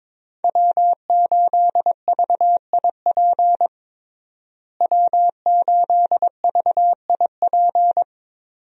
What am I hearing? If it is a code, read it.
Morse code, heard as a keyed tone, about 22 wpm, 700 Hz. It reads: W8VIP W8VIP
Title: USA Call Signs - 5 Characters - Repeated 2x - 22wpm